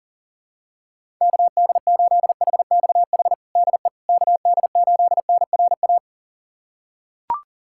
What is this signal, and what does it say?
Morse code, heard as a keyed tone, about 40 wpm, 700 Hz. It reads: KD8HXH DE KD8NRA
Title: USA Callsign DE Callsign 40wpm